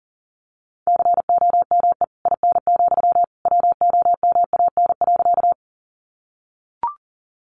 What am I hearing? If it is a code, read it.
Morse code, heard as a keyed tone, about 40 wpm, 700 Hz. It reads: COME IN, WOMAN.
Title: Sentences from Top 1100 Words 40wpm